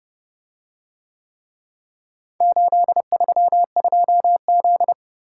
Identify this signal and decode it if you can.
Morse code, heard as a keyed tone, about 29 wpm, 700 Hz. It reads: 832Z